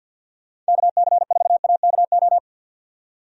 Morse code, heard as a keyed tone, about 51 wpm, 700 Hz. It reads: KC4AKO
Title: USA Call Signs - 6 Characters - 50wpm